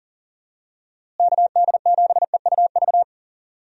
Morse code, heard as a keyed tone, about 40 wpm, 700 Hz. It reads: KD7EUV
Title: USA Call Signs - 6 Characters - Repeated 3x - 40wpm